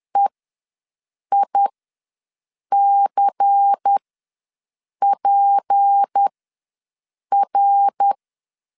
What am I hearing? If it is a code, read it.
Morse code, heard as a keyed tone, about 10 wpm, spaced as for 6 wpm, 800 Hz. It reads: EICPR
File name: LEZIONI / 6 PKX0 / 6 P ws 10wpm 3751